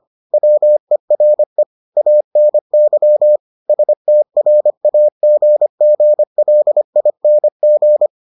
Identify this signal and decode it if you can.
Morse code, heard as a keyed tone, about 25 wpm, 600 Hz. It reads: WERE ANY STRAGGLING